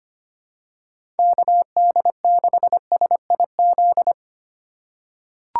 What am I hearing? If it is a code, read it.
Morse code, heard as a keyed tone, about 25 wpm, 700 Hz. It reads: KD6SIZ
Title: USA Call Signs - 6 Characters - 6x Character Spacing - 25wpm